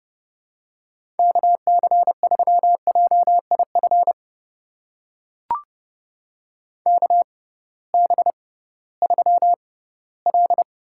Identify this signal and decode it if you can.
Morse code, heard as a keyed tone, about 30 wpm, spaced as for 17 wpm, 700 Hz. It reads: KC3JIF KB3L
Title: USA Call Signs - 6 Characters - 6x Character Spacing - 30wpm